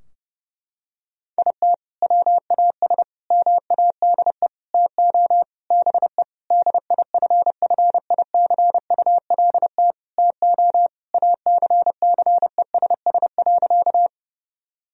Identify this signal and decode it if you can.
Morse code, heard as a keyed tone, about 30 wpm, 700 Hz. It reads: IT WAS MADE TO BE DIFFICULT TO ACCESS.
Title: Sentences from Top 1000 Words - Review Part 4 - 30wpm